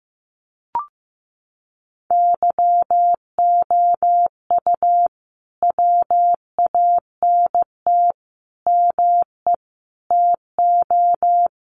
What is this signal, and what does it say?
Morse code, heard as a keyed tone, about 15 wpm, 700 Hz. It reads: YOU WANT ME TO
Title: Sets of 5 Words Encore2 15wpm